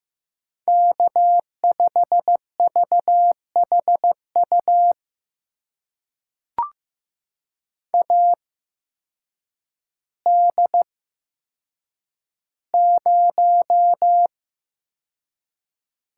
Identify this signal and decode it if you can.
Morse code, heard as a keyed tone, about 15 wpm, spaced as for 8 wpm, 700 Hz. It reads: K5VHU AD0
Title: USA Call Signs - 5 Characters - 8x Character Spacing - 15wpm